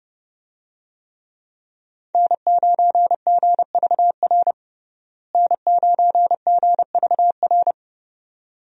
Morse code, heard as a keyed tone, about 30 wpm, 700 Hz. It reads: N9GVR N9GVR
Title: USA Call Signs - 5 Characters - Repeated 2x - 30wpm